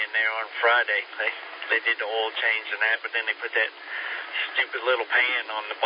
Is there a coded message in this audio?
Yes: Emergency Alert System (SAME) tones from 0.7 to 4.7 s, heard as harsh buzzing bursts, with speech.